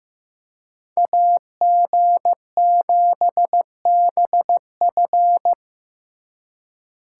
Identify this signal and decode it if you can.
Morse code, heard as a keyed tone, about 18 wpm, 700 Hz. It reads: AG7BF
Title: Speed Racing - USA 2x2 Call Signs 15wpm